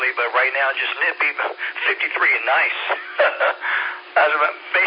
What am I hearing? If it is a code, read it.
Emergency Alert System (SAME) tones, heard as harsh buzzing bursts, with speech. It is heard from 0.4 to 4.0 s.